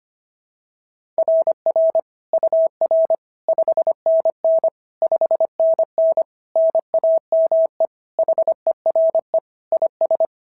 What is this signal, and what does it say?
Morse code, heard as a keyed tone, about 24 wpm, 650 Hz. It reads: RR UR 5NN 5NN NAME HERE IS
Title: SOTA QSO Practice 25wpm